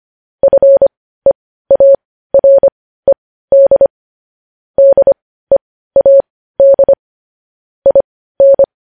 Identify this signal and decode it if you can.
Morse code, heard as a keyed tone, about 25 wpm, spaced as for 15 wpm, 550 Hz. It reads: FEARED DEAD IN
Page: news-headlines-in-morse-code-at-15-wpm